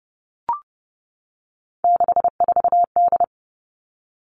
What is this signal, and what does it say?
Morse code, heard as a keyed tone, about 30 wpm, 700 Hz. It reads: 64D
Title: Three Challenging Letter-Number Combinations 30wpm